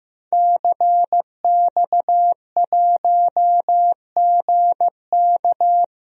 Morse code, heard as a keyed tone, about 15 wpm, 700 Hz. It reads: CX1GK
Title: DXCC-100 Call Signs - 5 Characters - 15wpm